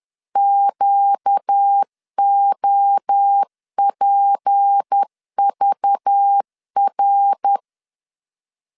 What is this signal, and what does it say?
Morse code, heard as a keyed tone, about 10 wpm, 800 Hz. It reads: QOPVR
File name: LEZIONI / 6 PKX0 / 6 PK 10wpm 2436